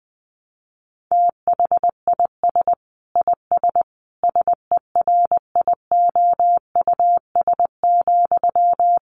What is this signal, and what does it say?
Morse code, heard as a keyed tone, about 20 wpm, 700 Hz. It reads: THIS IS SERIOUS,
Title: Sentences from Top 1100 Words 20wpm